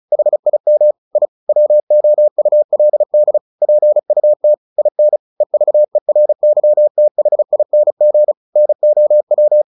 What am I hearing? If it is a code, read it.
Morse code, heard as a keyed tone, about 34 wpm, 600 Hz. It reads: HIM I WOULD PUT IN EVERYTHING NOW
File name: the old man and the sea / the old man and the sea 1 35wpm